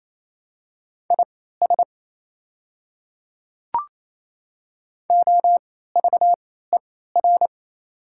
Morse code, heard as a keyed tone, about 28 wpm, spaced as for 16 wpm, 690 Hz. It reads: IS OVER
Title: 3x Character Spacing - Top 100 Words - 28wpm